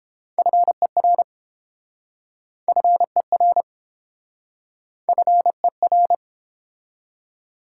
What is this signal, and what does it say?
Morse code, heard as a keyed tone, about 32 wpm, 700 Hz. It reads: FER FER FER